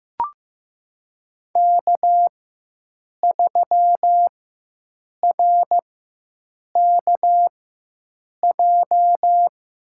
Morse code, heard as a keyed tone, about 15 wpm, spaced as for 7 wpm, 700 Hz. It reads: K3RKJ
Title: USA Call Signs - 5 Characters - 4x Character Spacing - 15wpm